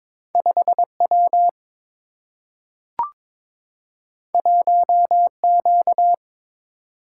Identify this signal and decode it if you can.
Morse code, heard as a keyed tone, about 23 wpm, 700 Hz. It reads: 5W 1Q
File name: practice / 22wpm / Two Letter-Number Combinations - 22wpm